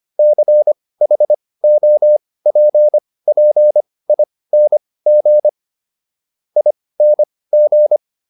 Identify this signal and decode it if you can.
Morse code, heard as a keyed tone, about 25 wpm, spaced as for 18 wpm, 600 Hz. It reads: CHOPPING ING